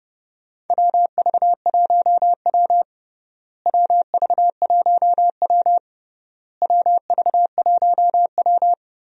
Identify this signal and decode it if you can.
Morse code, heard as a keyed tone, about 30 wpm, 700 Hz. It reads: WV1W WV1W WV1W
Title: USA Call Signs - 4 Characters - Repeated 3x - 30wpm